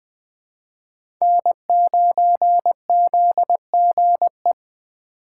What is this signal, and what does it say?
Morse code, heard as a keyed tone, about 20 wpm, 700 Hz. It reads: N9ZGE
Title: USA Call Signs - 5 Characters - 3x Character Spacing - 20wpm